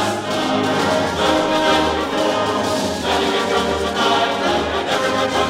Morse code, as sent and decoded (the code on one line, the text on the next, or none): none